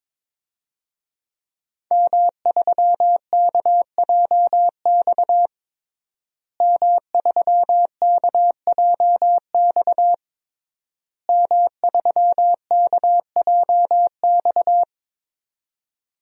Morse code, heard as a keyed tone, about 22 wpm, 700 Hz. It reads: M3KJX M3KJX M3KJX
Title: DXCC-100 Call Signs - 5 Characters - Repeated 3x - 22wpm